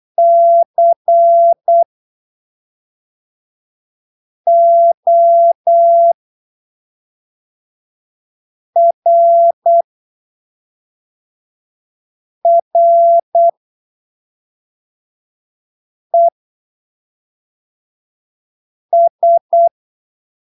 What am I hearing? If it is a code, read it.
Morse code, heard as a keyed tone, about 8 wpm, spaced as for 3 wpm, 680 Hz. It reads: CORRES
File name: arquivos / texto / texto 28 08ppm